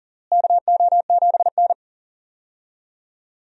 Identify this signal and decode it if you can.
Morse code, heard as a keyed tone, about 40 wpm, 700 Hz. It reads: KO7N
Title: USA Call Signs - 4 Characters - Repeated 3x - 40wpm